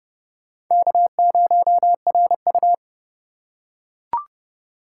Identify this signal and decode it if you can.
Morse code, heard as a keyed tone, about 31 wpm, 700 Hz. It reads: K0RU